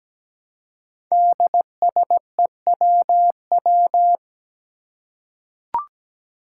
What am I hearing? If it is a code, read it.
Morse code, heard as a keyed tone, about 17 wpm, 700 Hz. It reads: DSEWW
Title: Five Challenging Letter-Number Combinations 17wpm